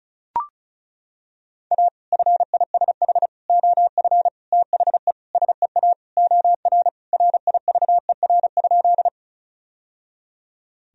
Morse code, heard as a keyed tone, about 36 wpm, 700 Hz. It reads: A FISH OF THE SEA OR RIVER?
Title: Sentences from Top 1000 Words - Review Part 1 - 35wpm